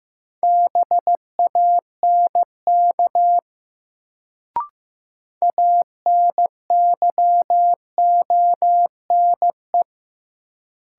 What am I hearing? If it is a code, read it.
Morse code, heard as a keyed tone, about 15 wpm, 700 Hz. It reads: BANK ANYONE
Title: Sentences from Top 900 Words - New Word Drill - 15wpm